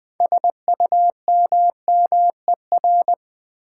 Morse code, heard as a keyed tone, about 20 wpm, 700 Hz. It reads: SUMMER
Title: Sentences from Top 1000 Words - Review Part 6 - New Word Drill - 20wpm